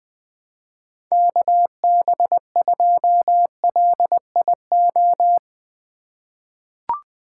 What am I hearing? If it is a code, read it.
Morse code, heard as a keyed tone, about 20 wpm, 700 Hz. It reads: KB2LIO
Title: USA Call Signs - 6 Characters - 2x Character Spacing - 20wpm